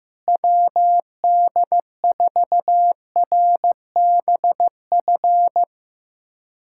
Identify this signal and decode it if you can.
Morse code, heard as a keyed tone, about 15 wpm, 700 Hz. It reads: WD4RBF
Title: USA Callsign DE Callsign - 6 Characters - 15wpm